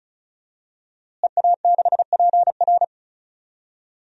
Morse code, heard as a keyed tone, about 34 wpm, 700 Hz. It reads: EA6PR
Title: DXCC-100 Call Signs - 5 Characters - Repeated 3x - 35wpm